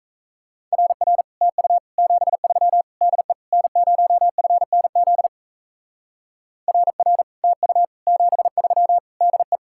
Morse code, heard as a keyed tone, about 36 wpm, 700 Hz. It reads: RR TU 73 DE N0FNZ RR TU 73 DE